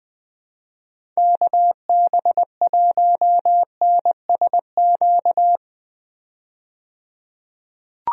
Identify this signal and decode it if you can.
Morse code, heard as a keyed tone, about 20 wpm, 700 Hz. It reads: KB1NSQ
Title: USA Call Signs - 6 Characters - 20wpm